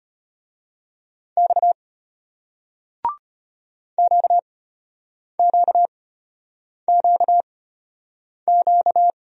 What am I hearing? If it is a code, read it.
Morse code, heard as a keyed tone, about 33 wpm, spaced as for 8 wpm, 700 Hz. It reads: X QQQQ